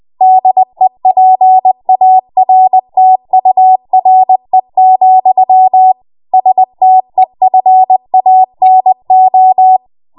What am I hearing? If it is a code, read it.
Morse code, heard as a keyed tone, about 20 wpm, 750 Hz. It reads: DEPARTURE, STEFANO
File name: Morse / 20WPM